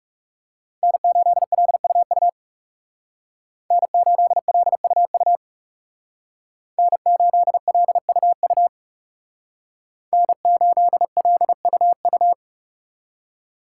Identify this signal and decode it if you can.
Morse code, heard as a keyed tone, about 35 wpm, 700 Hz. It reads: N8LUU N8LUU N8LUU N8LUU
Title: Speed Racing - USA 1x3 Call Signs 30wpm